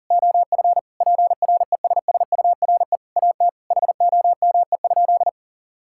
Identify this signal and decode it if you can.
Morse code, heard as a keyed tone, about 40 wpm, 690 Hz. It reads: OF PRESSURE AT HOME?
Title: Sentences from Top 1000 Words - Review Part 6 - 40wpm